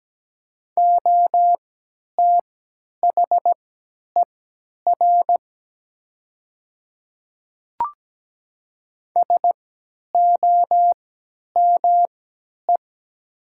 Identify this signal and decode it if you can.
Morse code, heard as a keyed tone, about 17 wpm, spaced as for 10 wpm, 700 Hz. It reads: OTHER SOME
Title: 3x Character Spacing - Top 100 Words - 17wpm